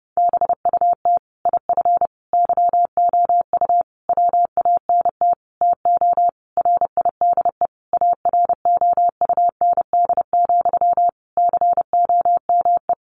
Morse code, heard as a keyed tone, about 30 wpm, 700 Hz. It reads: BUT IF YOU WANT TO RIDE AROUND, COME
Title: Sentences from Top 1200 Words 30wpm